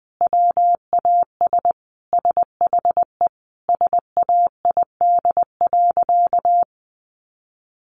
Morse code, heard as a keyed tone, about 20 wpm, 700 Hz. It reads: WAS SHE SAID.
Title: Sentences from Top 400 Words 20wpm